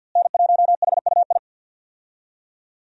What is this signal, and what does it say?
Morse code, heard as a keyed tone, about 52 wpm, 700 Hz. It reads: N1HUI
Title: USA Call Signs - 5 Characters - Repeated 3x - 50wpm